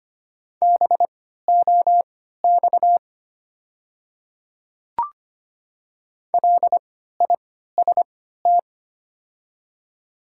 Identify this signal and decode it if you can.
Morse code, heard as a keyed tone, about 25 wpm, spaced as for 14 wpm, 690 Hz. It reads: BOX LIST
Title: 3x Character Spacing - Top 500 Words - 25wpm